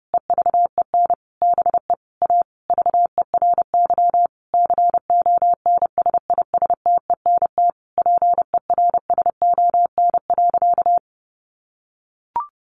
Morse code, heard as a keyed tone, about 30 wpm, 700 Hz. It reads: EVEN BE A VERY CONSISTENT PERSON.
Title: Sentences from Top 1500 Words 30wpm